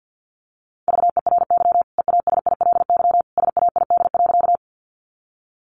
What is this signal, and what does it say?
Morse code, heard as a keyed tone, about 50 wpm, 700 Hz. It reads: VERY EASILY SAID.
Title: Sentences from Top 900 Words 50wpm